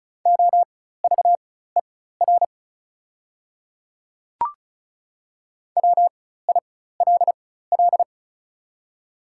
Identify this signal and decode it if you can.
Morse code, heard as a keyed tone, about 36 wpm, spaced as for 16 wpm, 690 Hz. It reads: OVER WILL
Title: 4x Character Spacing - Top 300 Words - 35wpm